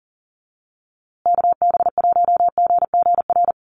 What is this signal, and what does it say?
Morse code, heard as a keyed tone, about 40 wpm, 700 Hz. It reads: KB1GGR
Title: USA Call Signs Repeated 3x 40wpm